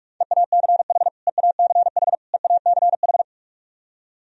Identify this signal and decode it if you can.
Morse code, heard as a keyed tone, about 46 wpm, 700 Hz. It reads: EACH EACH EACH
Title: Top 100 Words Repeated 45wpm